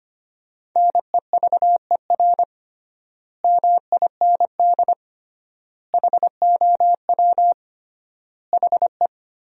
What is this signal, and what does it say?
Morse code, heard as a keyed tone, about 25 wpm, 700 Hz. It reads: NEVER MIND HOW HE
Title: 3x Word-Spacing - Sentences from Top 500 Words - 25wpm